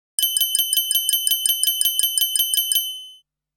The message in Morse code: none